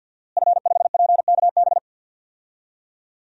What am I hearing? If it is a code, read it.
Morse code, heard as a keyed tone, about 51 wpm, 700 Hz. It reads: F5PKB
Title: DXCC-100 Call Signs - 5 Characters - Repeated 3x - 50wpm